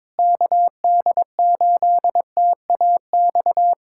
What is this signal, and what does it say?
Morse code, heard as a keyed tone, about 22 wpm, 700 Hz. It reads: KD8TAX
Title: USA Callsign DE Callsign 22wpm